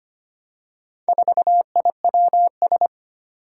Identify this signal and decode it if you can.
Morse code, heard as a keyed tone, about 25 wpm, 700 Hz. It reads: 4IWS
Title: Four Challenging Letter-Number Combinations 25wpm